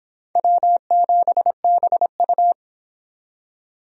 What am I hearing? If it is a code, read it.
Morse code, heard as a keyed tone, about 24 wpm, 700 Hz. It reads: W7BU